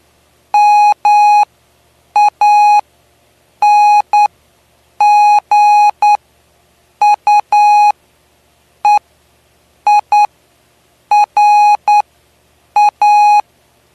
Morse code, heard as a keyed tone, about 10 wpm, spaced as for 7 wpm, 830 Hz. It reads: MANGUEIRA
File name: cw / py2lcd / 09-PY2LCD-Li��o 9